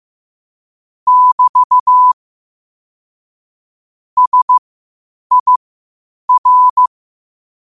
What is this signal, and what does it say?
Morse code, heard as a keyed tone, about 15 wpm, spaced as for 8 wpm, 1000 Hz. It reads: <BT> SIR
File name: qotdfiles / 8wpm